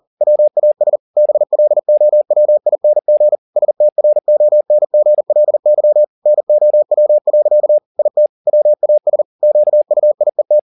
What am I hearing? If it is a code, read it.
Morse code, heard as a keyed tone, about 40 wpm, 600 Hz. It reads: WAS BLOWING STRONGLY NOW. IT WAS QUIET
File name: the old man and the sea 2 40wpm